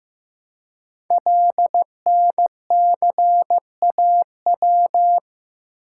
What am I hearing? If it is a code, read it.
Morse code, heard as a keyed tone, about 15 wpm, 700 Hz. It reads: LNCAW
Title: Code Groups - 5 Letters - 15wpm